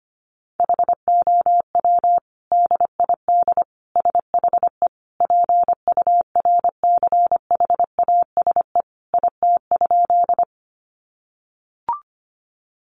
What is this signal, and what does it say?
Morse code, heard as a keyed tone, about 25 wpm, 700 Hz. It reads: HOW DID SHE PURCHASE IT?
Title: Sentences from Top 800 Words 25wpm